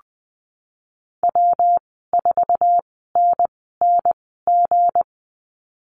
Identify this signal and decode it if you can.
Morse code, heard as a keyed tone, about 20 wpm, spaced as for 15 wpm, 700 Hz. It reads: W4NNG
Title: 193 Lesson 31g - Letter v - Call Signs 20wpm